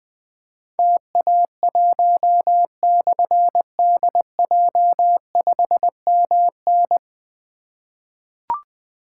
Morse code, heard as a keyed tone, about 20 wpm, 700 Hz. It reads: TA1/DJ5MN
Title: Difficult Call Signs 20wpm